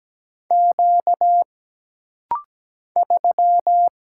Morse code, heard as a keyed tone, about 17 wpm, 700 Hz. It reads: Q 3
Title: Single Letter-Number - Rapid-Fire 17wpm